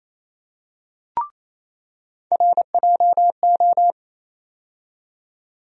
Morse code, heard as a keyed tone, about 27 wpm, 700 Hz. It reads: RJO
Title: USA Call Sign Postfixes - 3 Letters 28wpm